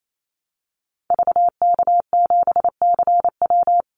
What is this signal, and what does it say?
Morse code, heard as a keyed tone, about 28 wpm, 700 Hz. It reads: VK7CW
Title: Call Signs Repeated 2x - 28wpm